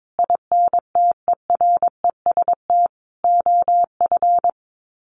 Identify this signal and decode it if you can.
Morse code, heard as a keyed tone, about 22 wpm, 700 Hz. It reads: INTEREST OF
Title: Sets of 5 Words Encore 22wpm